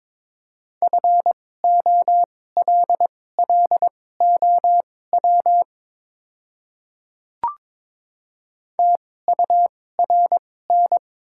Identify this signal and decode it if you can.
Morse code, heard as a keyed tone, about 22 wpm, spaced as for 16 wpm, 700 Hz. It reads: FOLLOW TURN
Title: 2x Character Spacing - Top 200 Words - 22wpm